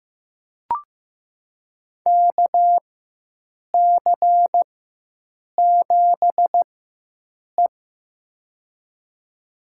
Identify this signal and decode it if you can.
Morse code, heard as a keyed tone, about 15 wpm, spaced as for 7 wpm, 700 Hz. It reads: KC7E
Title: USA Call Signs - 4x Character Spacing - 15wpm